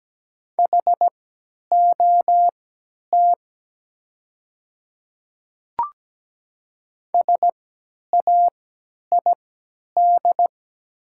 Morse code, heard as a keyed tone, about 17 wpm, spaced as for 10 wpm, 700 Hz. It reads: HOT SAID
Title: 3x Character Spacing - Top 100 Words - 17wpm